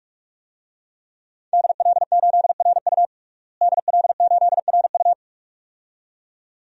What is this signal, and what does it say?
Morse code, heard as a keyed tone, about 42 wpm, 700 Hz. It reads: DL8RU DL8RU
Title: DXCC-100 Call Signs - 5 Characters - Repeated 2x - 45wpm